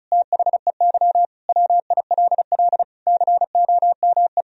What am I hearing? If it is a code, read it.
Morse code, heard as a keyed tone, about 34 wpm, 700 Hz. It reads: THEY WILL COME
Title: Sentences from Top 1000 Words - Review Part 1 - 35wpm